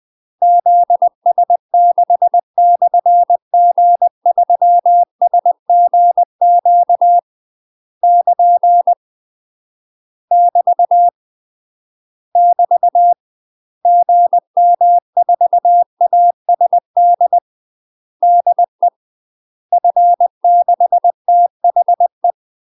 Morse code, heard as a keyed tone, about 20 wpm, 700 Hz. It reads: ZS6/G3SGQ <KN> <BT> <BT> GM4ASD DE F6THE
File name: UK MASTER QSOs TRAINING / 20WPM / UK MASTER QSOs 0029